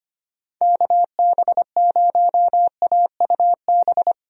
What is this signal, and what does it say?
Morse code, heard as a keyed tone, about 25 wpm, 700 Hz. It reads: KB0AUB
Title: USA Call Signs - 6 Characters - Repeated 3x - 25wpm